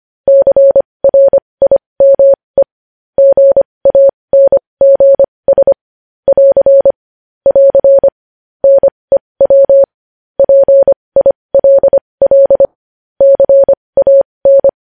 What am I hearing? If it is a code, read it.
Morse code, heard as a keyed tone, about 25 wpm, spaced as for 20 wpm, 550 Hz. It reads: CRIME GANGS <AR> <AR> NEW PILL CAN
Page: News Headlines in Morse Code at 20 WPM